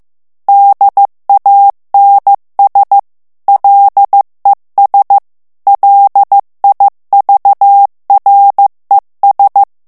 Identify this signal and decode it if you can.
Morse code, heard as a keyed tone, about 15 wpm, 800 Hz. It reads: DANS LES LIVRES
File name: FRANCAIS TEXTE CLAIR 2 VITESSE 14 MOTS MINUTES